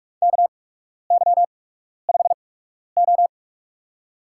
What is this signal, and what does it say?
Morse code, heard as a keyed tone, about 46 wpm, spaced as for 12 wpm, 700 Hz. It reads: KY5O
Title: USA Call Signs - 4 Characters - 8x Character Spacing - 45wpm